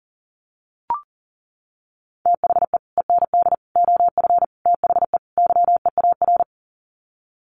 Morse code, heard as a keyed tone, about 40 wpm, 700 Hz. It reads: THE END OF THE YEAR
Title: Sets of 5 Words 40wpm